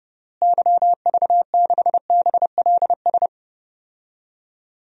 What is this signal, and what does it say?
Morse code, heard as a keyed tone, about 30 wpm, 700 Hz. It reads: YV6BLS
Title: DXCC-100 Call Signs - 6 Characters - Repeated 2x - 30wpm